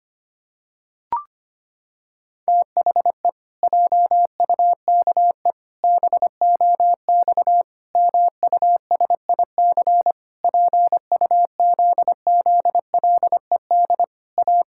Morse code, heard as a keyed tone, about 25 wpm, 700 Hz. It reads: THE JUKE BOX MUSIC PUZZLED A
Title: Pangram Challenge v2 - 25wpm